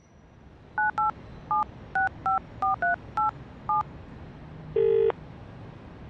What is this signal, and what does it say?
DTMF, heard as telephone touch tones, tones of about 140 ms. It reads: 987654387
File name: Hot English MP3s / he-74-mp3s / HE 75 - trk 45 Telephone Conversatio 2